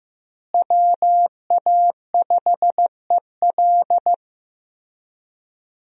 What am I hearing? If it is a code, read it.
Morse code, heard as a keyed tone, about 15 wpm, 700 Hz. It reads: WA5EL
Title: USA Call Signs - 5 Characters - Repeated 3x - 15wpm